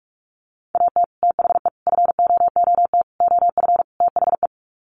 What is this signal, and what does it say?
Morse code, heard as a keyed tone, about 44 wpm, 700 Hz. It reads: AT THE FOOT OF THE
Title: Sets of 5 Words 45wpm